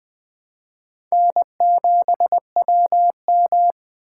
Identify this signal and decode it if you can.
Morse code, heard as a keyed tone, about 21 wpm, 700 Hz. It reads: N7WM
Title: 2023 SOTA Chaser Call Signs - 20wpm